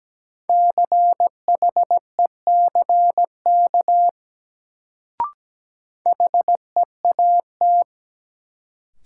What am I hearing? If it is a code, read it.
Morse code, heard as a keyed tone, about 17 wpm, 700 Hz. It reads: CHECK HEAT
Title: Sentences from Top 1000 Words - Review Part 3 - New Word Drill - 17wpm